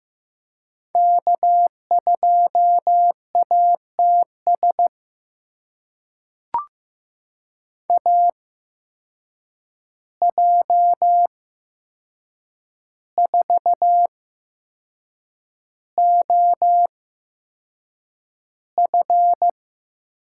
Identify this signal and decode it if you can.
Morse code, heard as a keyed tone, about 15 wpm, spaced as for 6 wpm, 700 Hz. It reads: K2ATS AJ4OF
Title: USA Call Signs - 5 Characters - 8x Character Spacing - 15wpm